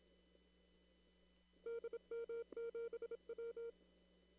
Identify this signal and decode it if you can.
Morse code, heard as a keyed tone, about 29 wpm, 460 Hz. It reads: DM7W
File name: CW / kt5j wpx cw-op-k5pi